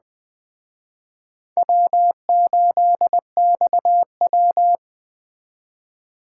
Morse code, heard as a keyed tone, about 19 wpm, 700 Hz. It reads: W8XW